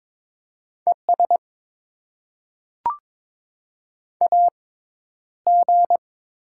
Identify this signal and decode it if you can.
Morse code, heard as a keyed tone, about 22 wpm, 700 Hz. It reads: E S A G
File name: Top 20 QSO Elements - 6x Character Spacing - 22wpm